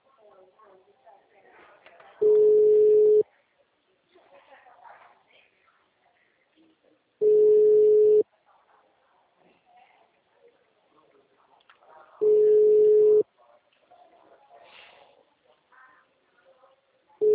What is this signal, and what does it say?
Ringback tone (Europe, 425 Hz), with speech.
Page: Limeira - SP